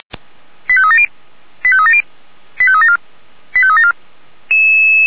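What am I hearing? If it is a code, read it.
Five-tone selective call (ZVEI), heard as quick runs of five tones, tones of about 70 ms. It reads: 85370 85370 85374 85374
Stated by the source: ZVEI five-tone call 89371 received by SDR.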